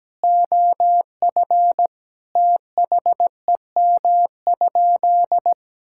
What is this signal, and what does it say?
Morse code, heard as a keyed tone, about 17 wpm, 700 Hz. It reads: OF THEM?
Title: Sentences from Top 1000 Words - Review Part 6 - 17wpm